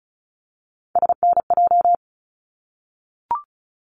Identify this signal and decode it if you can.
Morse code, heard as a keyed tone, about 36 wpm, 700 Hz. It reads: SNJ
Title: ARRL Sections 35wpm